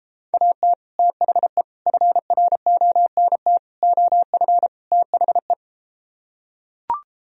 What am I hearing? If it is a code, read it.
Morse code, heard as a keyed tone, about 33 wpm, 700 Hz. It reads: AT THE FRONT OF THE